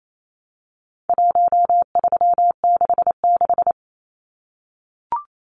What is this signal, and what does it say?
Morse code, heard as a keyed tone, about 28 wpm, 700 Hz. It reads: 1366
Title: Numbers - Contest Style - 4-Digit 28wpm